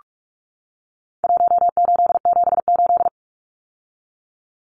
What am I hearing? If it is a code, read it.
Morse code, heard as a keyed tone, about 44 wpm, 700 Hz. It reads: 1878
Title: Numbers - Contest Style - 4-Digit 45wpm